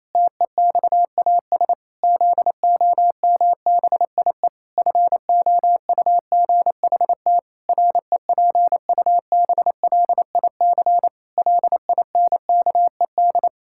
Morse code, heard as a keyed tone, about 27 wpm, 700 Hz. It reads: TEXAS ZOMBIE FOUGHT REPUBLIC LINKED
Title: Pangram Challenge v2 - 28wpm